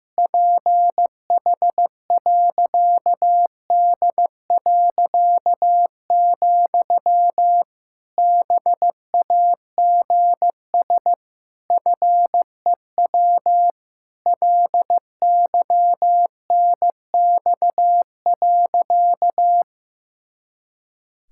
Morse code, heard as a keyed tone, about 15 wpm, 700 Hz. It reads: PH.D., BAGS FEW LYNX.
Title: Pangram Challenge v2 - 15wpm